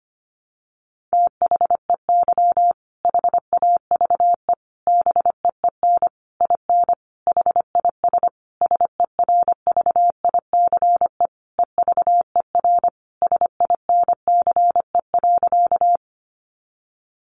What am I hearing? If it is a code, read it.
Morse code, heard as a keyed tone, about 25 wpm, 700 Hz. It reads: THEY HAVE BEEN IN HIS SERVICE EVER SINCE.
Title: Sentences from Top 700 Words 25wpm